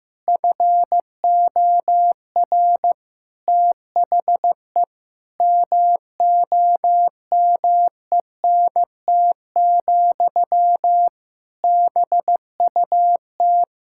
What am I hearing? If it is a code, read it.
Morse code, heard as a keyed tone, about 15 wpm, 700 Hz. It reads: FOR THE MOMENT, BUT
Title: Sentences from Top 1000 Words - Review Part 6 - 15wpm